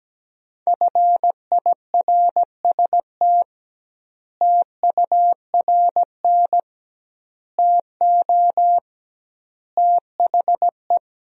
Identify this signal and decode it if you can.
Morse code, heard as a keyed tone, about 17 wpm, 700 Hz. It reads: FIRST TURN TO THE
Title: 2x Word-Spacing - Sentences from Top 300 Words - 17wpm